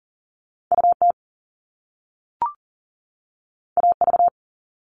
Morse code, heard as a keyed tone, about 39 wpm, 700 Hz. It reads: UT AV
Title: Top Two Letter Combinations in Words 40wpm